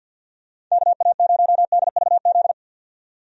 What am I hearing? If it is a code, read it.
Morse code, heard as a keyed tone, about 52 wpm, 700 Hz. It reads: KA0DVZ
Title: USA Call Signs - 6 Characters - 50wpm